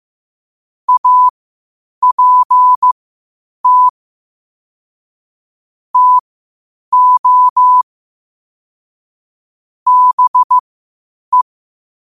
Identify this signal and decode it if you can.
Morse code, heard as a keyed tone, about 15 wpm, spaced as for 8 wpm, 1000 Hz. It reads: APT TO BE
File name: qotdfiles / 8wpm